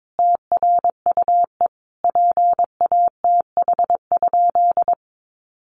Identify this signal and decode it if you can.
Morse code, heard as a keyed tone, about 22 wpm, 700 Hz. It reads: TRUE PATH?
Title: Sentences from Top 1400 Words 22wpm